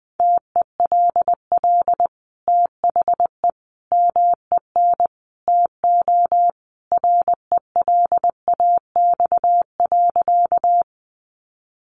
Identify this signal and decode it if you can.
Morse code, heard as a keyed tone, about 20 wpm, 700 Hz. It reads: TELL THE MEN TO RELAX.
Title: Sentences from Top 1400 Words 20wpm